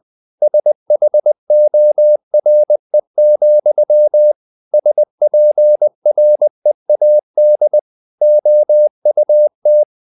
Morse code, heard as a keyed tone, about 20 wpm, 600 Hz. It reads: SHORE, SPREAD OUT
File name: the old man and the sea / the old man and the sea part 1 20wpm